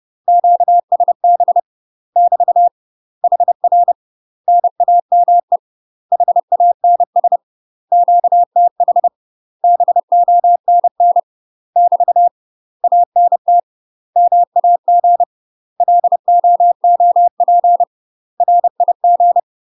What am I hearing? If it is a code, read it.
Morse code, heard as a keyed tone, about 30 wpm, 700 Hz. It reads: QSB <BT> HR NAME HANS QTH BONN <BT> ANT MAG LOOP RIG
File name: UK MASTER QSOs TRAINING / 30WPM / UK MASTER QSOs 0001